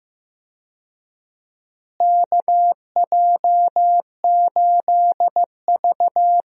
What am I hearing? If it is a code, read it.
Morse code, heard as a keyed tone, about 16 wpm, 700 Hz. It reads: KJ8V
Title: USA Call Signs - 4 Characters - Repeated 3x - 15wpm